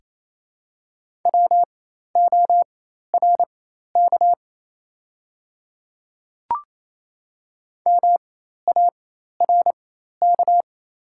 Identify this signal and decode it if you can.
Morse code, heard as a keyed tone, about 27 wpm, spaced as for 13 wpm, 690 Hz. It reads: WORK MARK
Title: 4x Character Spacing - Top 300 Words - 28wpm